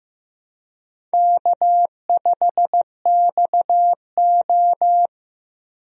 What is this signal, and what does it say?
Morse code, heard as a keyed tone, about 15 wpm, 700 Hz. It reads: K5XO
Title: USA Call Signs - 4 Characters - Repeated 2x - 15wpm